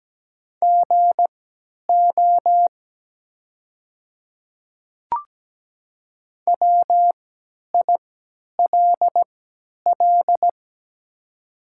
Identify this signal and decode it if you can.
Morse code, heard as a keyed tone, about 17 wpm, spaced as for 10 wpm, 700 Hz. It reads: GO WILL
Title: 3x Character Spacing - Top 200 Words - 17wpm